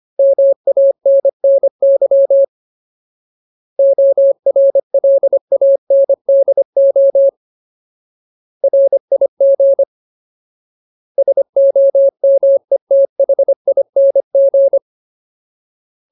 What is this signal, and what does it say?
Morse code, heard as a keyed tone, about 25 wpm, 550 Hz. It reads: MANNY ORLANDO RIG SOMETHING
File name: Ultimate Word List / 25WPM Ultimate